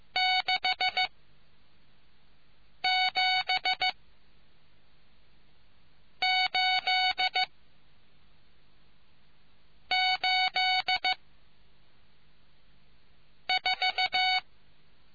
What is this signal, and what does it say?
Morse code, heard as a keyed tone, about 15 wpm, spaced as for 4 wpm, 740 Hz. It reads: 67884